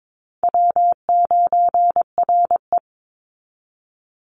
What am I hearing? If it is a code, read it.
Morse code, heard as a keyed tone, about 22 wpm, 700 Hz. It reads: W9RE
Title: Easy Call Signs Repeated 3x - 22wpm